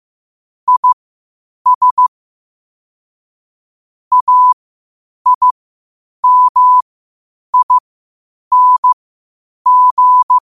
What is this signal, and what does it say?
Morse code, heard as a keyed tone, about 15 wpm, spaced as for 8 wpm, 1000 Hz. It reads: IS AIMING
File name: qotdfiles / 8wpm